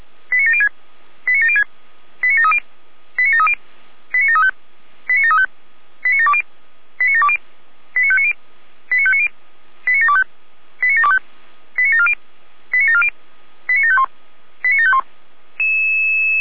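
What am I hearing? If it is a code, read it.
Five-tone selective call (ZVEI), heard as quick runs of five tones, tones of about 70 ms. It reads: 89706 89706 89730 89730 89735 89735 89720 89720 89590 89590 89725 89725 89740 89740 89751 89751
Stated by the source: ZVEI five-tone call 89752 received by SDR.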